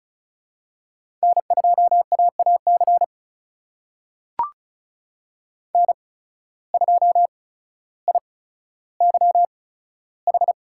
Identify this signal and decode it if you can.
Morse code, heard as a keyed tone, about 34 wpm, spaced as for 15 wpm, 700 Hz. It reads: N2AAC N2IYH
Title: USA Call Signs - 5 Characters - 8x Character Spacing - 35wpm